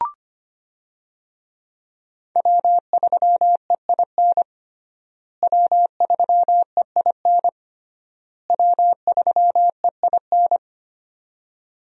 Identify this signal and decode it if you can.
Morse code, heard as a keyed tone, about 25 wpm, 700 Hz. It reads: W3EIN W3EIN W3EIN
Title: USA Call Signs - 5 Characters - Repeated 3x - 25wpm